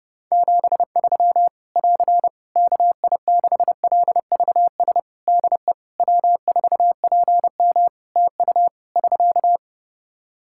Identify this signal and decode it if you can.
Morse code, heard as a keyed tone, about 30 wpm, 700 Hz. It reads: 73 <AR> KI6LVS DE W4PM TU <SK>
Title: Practice QSO - Exchange 4 - Ending - 30wpm